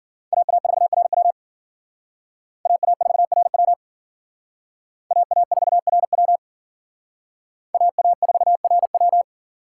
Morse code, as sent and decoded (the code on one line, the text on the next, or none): .- .- ....- .-. .-- / .- .- ....- .-. .-- / .- .- ....- .-. .-- / .- .- ....- .-. .--
AA4RW AA4RW AA4RW AA4RW